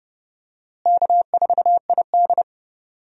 Morse code, heard as a keyed tone, about 31 wpm, 700 Hz. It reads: K4ID
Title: USA Call Signs - 4 Characters - Repeated 2x - 30wpm